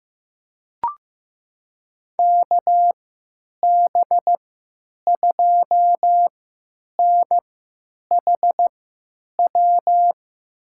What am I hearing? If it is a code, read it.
Morse code, heard as a keyed tone, about 16 wpm, spaced as for 9 wpm, 700 Hz. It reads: KB2NHW
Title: USA Call Signs - 6 Characters - 3x Character Spacing - 15wpm